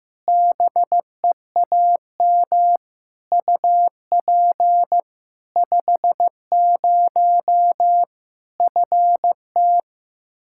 Morse code, heard as a keyed tone, about 15 wpm, 700 Hz. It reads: BEAM UP 50 FT
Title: K1USN SST QSO Practice 15wpm